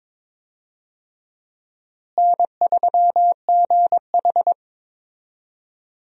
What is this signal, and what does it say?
Morse code, heard as a keyed tone, about 22 wpm, 700 Hz. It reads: N3GH
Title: USA Call Signs - 4 Characters - Repeated 3x - 22wpm